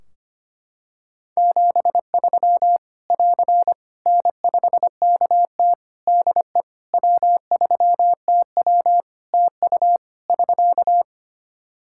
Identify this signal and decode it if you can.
Morse code, heard as a keyed tone, about 25 wpm, 700 Hz. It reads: 73 <AR> N5KT DE W3TW TU <SK>
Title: Practice QSO - Exchange 4 - Ending - 25wpm